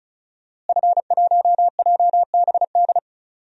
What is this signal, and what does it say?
Morse code, heard as a keyed tone, about 36 wpm, 700 Hz. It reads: F1JBD
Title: DXCC-100 Call Signs - 5 Characters - Repeated 3x - 35wpm